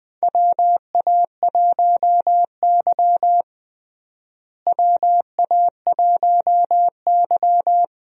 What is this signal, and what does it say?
Morse code, heard as a keyed tone, about 21 wpm, 700 Hz. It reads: WA1Y WA1Y
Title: USA Call Signs - 4 Characters - Repeated 2x - 20wpm